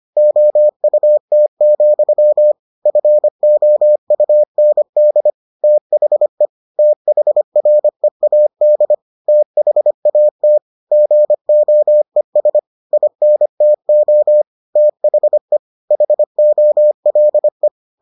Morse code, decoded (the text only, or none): OUT, FOUND THE THREAD THAT GOES INTO THE HOLE